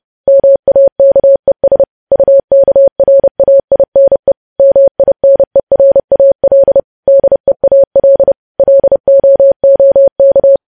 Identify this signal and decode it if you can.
Morse code, heard as a keyed tone, about 30 wpm, 550 Hz. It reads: MAKES UKRAINE MINERAL DEAL LOOK
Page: News Headlines in Morse Code at 30 WPM